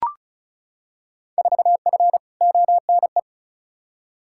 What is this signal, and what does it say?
Morse code, heard as a keyed tone, about 36 wpm, 700 Hz. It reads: 4F ONE
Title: Field Day Exchanges Encore - 35wpm